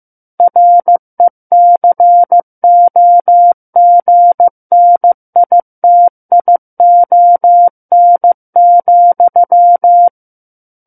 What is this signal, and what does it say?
Morse code, heard as a keyed tone, about 15 wpm, 700 Hz. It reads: RECOGNITION,
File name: Book One / 06-Chapter 6 - 15 WPM